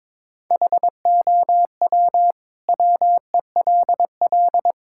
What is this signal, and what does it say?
Morse code, heard as a keyed tone, about 22 wpm, 700 Hz. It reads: HOW WELL
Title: Sentences from Top 1000 Words - Review Part 6 - 22wpm